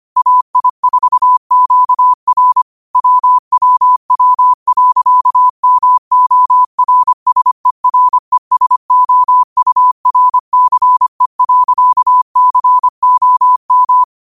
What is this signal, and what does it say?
Morse code, heard as a keyed tone, about 25 wpm, 1000 Hz. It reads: AI4QR WWW.MORSERESOURCE.COM
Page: Quotes for Fri, 15 Aug 2025 in Morse Code at 25 words per minute.